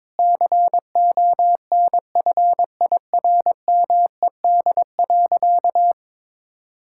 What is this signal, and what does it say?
Morse code, heard as a keyed tone, about 22 wpm, 700 Hz. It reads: CONFIRMED.
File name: practice / 22wpm / Sentences with 8-Word Phrases - 22wpm